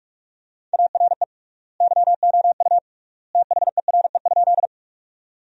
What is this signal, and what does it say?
Morse code, heard as a keyed tone, about 46 wpm, 700 Hz. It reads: ARE YOU THERE?
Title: 3x Word-Spacing - Sentences from Top 100 Words - 45wpm